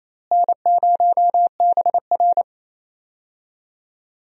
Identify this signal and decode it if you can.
Morse code, heard as a keyed tone, about 28 wpm, 700 Hz. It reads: N0BR
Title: USA Call Signs - 4 Characters - Repeated 3x - 28wpm